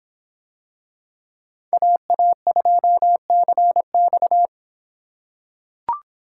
Morse code, heard as a keyed tone, about 24 wpm, 700 Hz. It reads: AA2CX